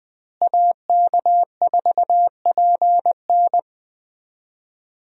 Morse code, heard as a keyed tone, about 20 wpm, 700 Hz. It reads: AK4PN
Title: USA Call Signs - 5 Characters - Repeated 2x - 20wpm